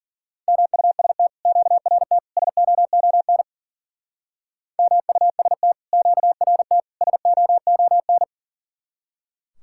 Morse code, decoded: MUST QRT SOON MUST QRT SOON

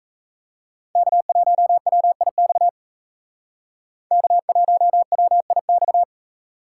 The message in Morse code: K1WIX K1WIX

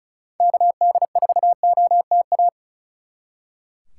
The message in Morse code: KD4OTA

